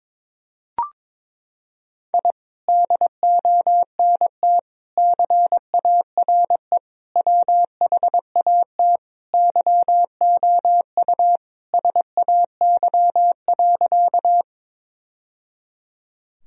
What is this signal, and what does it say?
Morse code, heard as a keyed tone, about 22 wpm, 700 Hz. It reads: I DONT CARE WHAT YOU SAY.
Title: Sentences from Top 1000 Words - Review Part 1 - 22wpm